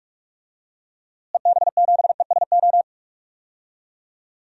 Morse code, heard as a keyed tone, about 41 wpm, 700 Hz. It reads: EB7ESO
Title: DXCC-100 Call Signs - 6 Characters - Repeated 2x - 45wpm